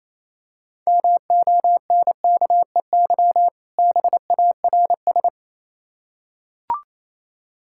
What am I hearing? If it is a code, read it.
Morse code, heard as a keyed tone, about 26 wpm, 700 Hz. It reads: MONKEY BARS
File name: Easy Words - Speed-Racing with Hints - 22wpm